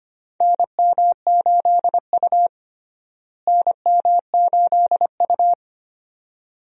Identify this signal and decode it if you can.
Morse code, heard as a keyed tone, about 25 wpm, 700 Hz. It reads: NM8U NM8U
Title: USA Call Signs - 4 Characters - Repeated 2x - 25wpm